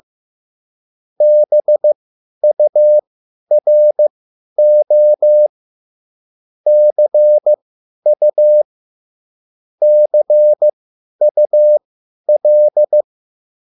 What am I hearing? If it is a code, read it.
Morse code, heard as a keyed tone, about 15 wpm, spaced as for 10 wpm, 600 Hz. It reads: BURO CU CUL
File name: QSL / CW lesson